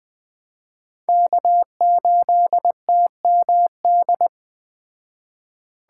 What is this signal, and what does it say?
Morse code, heard as a keyed tone, about 20 wpm, 700 Hz. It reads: K8TMD